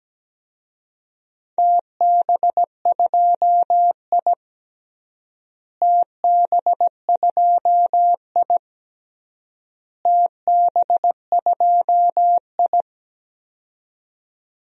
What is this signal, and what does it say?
Morse code, heard as a keyed tone, about 17 wpm, 700 Hz. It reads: TB2I TB2I TB2I
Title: DXCC-100 Call Signs - 4 Characters - Repeated 3x - 17wpm